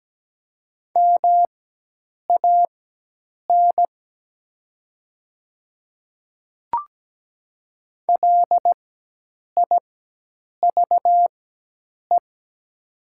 Morse code, heard as a keyed tone, about 17 wpm, spaced as for 8 wpm, 700 Hz. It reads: MAN LIVE
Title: 4x Character Spacing - Top 200 Words - 17wpm